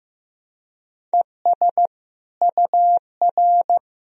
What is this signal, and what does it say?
Morse code, heard as a keyed tone, about 15 wpm, 700 Hz. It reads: ES UR
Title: Sentences from Top 300 Words and Abbreviations 15wpm